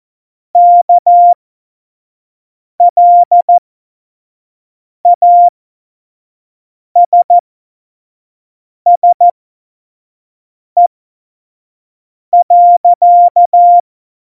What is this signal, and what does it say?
Morse code, heard as a keyed tone, about 14 wpm, spaced as for 5 wpm, 700 Hz. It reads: KLASSE.
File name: Morsekurs / Witze / Witz 04. Juni 2019, 25 F70